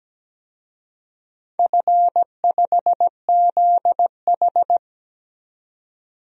Morse code, heard as a keyed tone, about 17 wpm, 700 Hz. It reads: F5ZH
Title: DXCC-100 Call Signs - 4 Characters - 17wpm